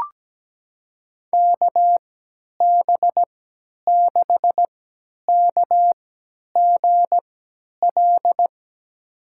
Morse code, heard as a keyed tone, about 17 wpm, spaced as for 10 wpm, 700 Hz. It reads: KB6KGL
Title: USA Call Signs - 6 Characters - 3x Character Spacing - 17wpm